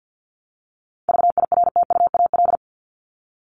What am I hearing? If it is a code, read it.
Morse code, heard as a keyed tone, about 50 wpm, 700 Hz. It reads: VIRTUAL
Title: LY Suffix - Top 100 Words with Base Words 50wpm